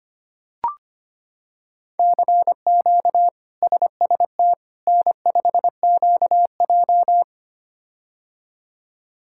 Morse code, heard as a keyed tone, about 25 wpm, 700 Hz. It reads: CQ SST N5QJ
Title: K1USN SST QSO Practice - 25wpm